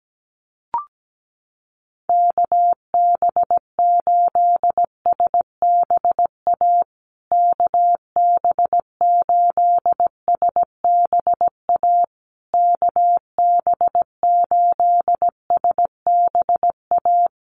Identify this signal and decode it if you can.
Morse code, heard as a keyed tone, about 17 wpm, 700 Hz. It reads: KB8SBA KB8SBA KB8SBA
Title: USA Call Signs Repeated 3x 17wpm